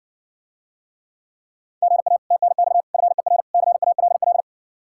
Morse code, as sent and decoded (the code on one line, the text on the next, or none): --. .- / - -. -..- / ..-. .-. / -.-. .- .-.. .-..
GA TNX FER CALL